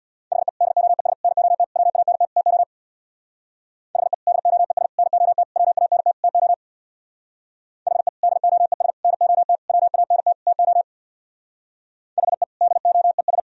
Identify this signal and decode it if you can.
Morse code, heard as a keyed tone, about 63 wpm, 700 Hz. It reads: HE DOES NOT WANT TO HE DOES NOT WANT TO HE DOES NOT WANT TO HE DOES